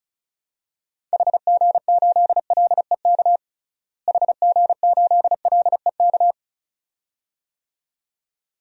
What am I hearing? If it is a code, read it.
Morse code, heard as a keyed tone, about 34 wpm, 700 Hz. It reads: HG8LEK HG8LEK
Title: DXCC-100 Call Signs - 6 Characters - Repeated 2x - 35wpm